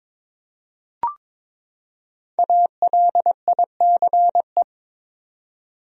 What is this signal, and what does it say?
Morse code, heard as a keyed tone, about 22 wpm, 700 Hz. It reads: ALICE
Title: USA Common Names - Max 5 Letters - 22wpm